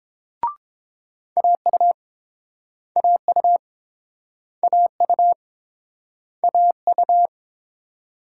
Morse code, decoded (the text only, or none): AU AU AU AU